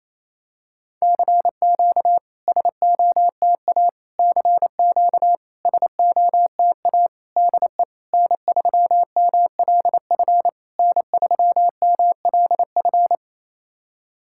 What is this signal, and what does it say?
Morse code, heard as a keyed tone, about 28 wpm, 700 Hz. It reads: CQ SOTA CQ SOTA DE N3MLF N3MLF
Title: SOTA QSO Practice 28wpm